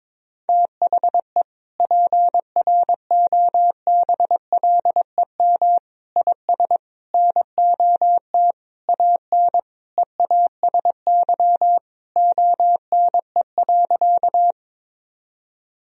Morse code, heard as a keyed tone, about 22 wpm, 700 Hz. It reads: THE PROBLEM IS NOT AN EASY ONE.
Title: Sentences from Top 1000 Words - Review Part 2 - 22wpm